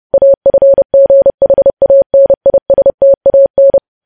Morse code, heard as a keyed tone, about 30 wpm, 550 Hz. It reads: AFGHANISTAN